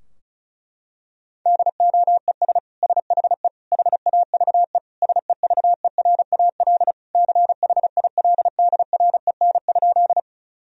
Morse code, heard as a keyed tone, about 36 wpm, 700 Hz. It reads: DOES SHE HAVE SEVERAL CHILDREN?
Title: Sentences from Top 1000 Words - Review Part 2 - 35wpm